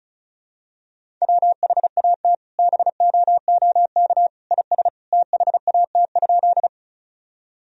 Morse code, heard as a keyed tone, about 36 wpm, 700 Hz. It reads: WHAT BOOK IS THAT?
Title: Sentences from Top 1000 Words - Review Part 1 - 35wpm